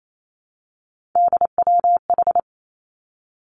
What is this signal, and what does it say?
Morse code, heard as a keyed tone, about 28 wpm, 700 Hz. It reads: DWH
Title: Three Challenging Letter-Number Combinations 28wpm